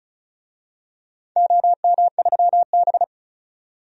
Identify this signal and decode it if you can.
Morse code, heard as a keyed tone, about 34 wpm, 700 Hz. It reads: OM3B
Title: DXCC-100 Call Signs - 4 Characters - 35wpm